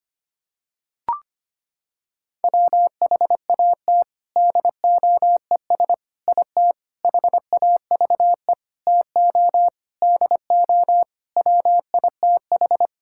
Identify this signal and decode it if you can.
Morse code, heard as a keyed tone, about 25 wpm, 700 Hz. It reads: WHAT DOES IT HAVE TO DO WITH
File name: practice / 25wpm / Sentences with 4-Word Phrases - 25wpm